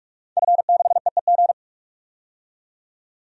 Morse code, heard as a keyed tone, about 44 wpm, 700 Hz. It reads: F6EEG
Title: DXCC-100 Call Signs - 5 Characters - 45wpm